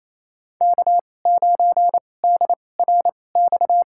Morse code, heard as a keyed tone, about 28 wpm, spaced as for 20 wpm, 700 Hz. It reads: K9DRX
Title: USA Call Signs - 5 Characters - 2x Character Spacing - 28wpm